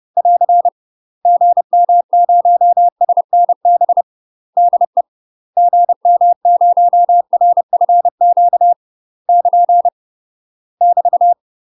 Morse code, heard as a keyed tone, about 30 wpm, 700 Hz. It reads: <AR> GM0SNB DE GM0RFQ <KN> <BT>
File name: UK MASTER QSOs TRAINING / 30WPM / UK MASTER QSOs 0042